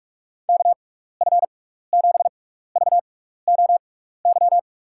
Morse code, heard as a keyed tone, about 46 wpm, spaced as for 15 wpm, 700 Hz. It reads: KF7VOY
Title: USA Call Signs - 6 Characters - 6x Character Spacing - 45wpm